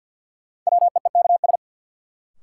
Morse code, heard as a keyed tone, about 52 wpm, 700 Hz. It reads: WEEKS
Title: Sets of 5 Words - New Word Drill - 50wpm